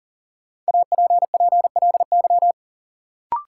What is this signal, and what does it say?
Morse code, heard as a keyed tone, about 40 wpm, 700 Hz. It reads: APPLY
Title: Sentences from Top 1200 Words - New Word Drill Alt - 40wpm